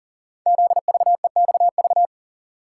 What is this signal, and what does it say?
Morse code, heard as a keyed tone, about 42 wpm, 690 Hz. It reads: ZVEXV
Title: Code Groups - 5 Letters - 40wpm